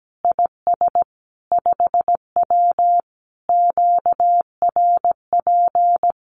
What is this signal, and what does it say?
Morse code, heard as a keyed tone, about 17 wpm, 700 Hz. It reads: IS 5W QRP
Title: Common QSO Phrases 17wpm